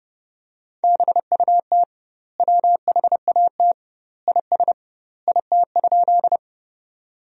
Morse code, heard as a keyed tone, about 30 wpm, 700 Hz. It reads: BUT WHAT IS IT?
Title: 2x Word-Spacing - Sentences from Top 100 Words - 30wpm